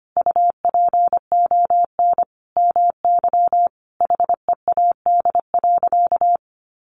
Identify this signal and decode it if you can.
Morse code, heard as a keyed tone, about 25 wpm, 700 Hz. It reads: UPON MY HEAD.
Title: Sentences from Top 1100 Words 25wpm